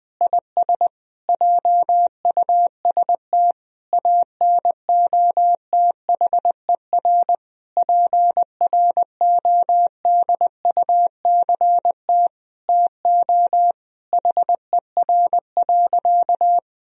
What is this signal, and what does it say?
Morse code, heard as a keyed tone, about 21 wpm, 700 Hz. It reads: IS JUST ANOTHER PRODUCT TO HER.
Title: Sentences from Top 1000 Words - Review Part 3 - 20wpm